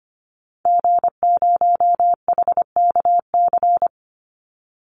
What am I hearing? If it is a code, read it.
Morse code, heard as a keyed tone, about 25 wpm, 700 Hz. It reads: G0HKC
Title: Call Signs Repeated 2x - 25wpm